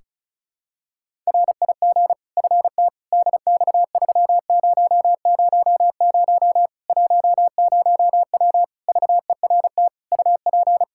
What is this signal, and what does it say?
Morse code, heard as a keyed tone, about 36 wpm, 700 Hz. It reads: RIG FT DX3000 10W VERT UP
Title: Practice QSO - Exchange 2 - Info - 35wpm